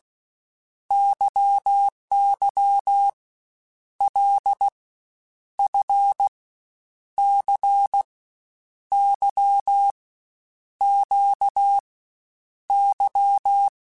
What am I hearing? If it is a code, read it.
Morse code, heard as a keyed tone, about 16 wpm, 800 Hz. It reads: YY L F C Y Q Y
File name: Basic Letter Groups By7Groups / LFCYQ / LFCYQ@13